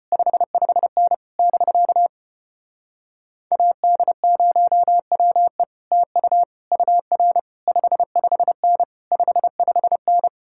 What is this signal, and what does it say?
Morse code, heard as a keyed tone, about 33 wpm, 700 Hz. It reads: 55N <BK> AD0WE TU UR 55N 55N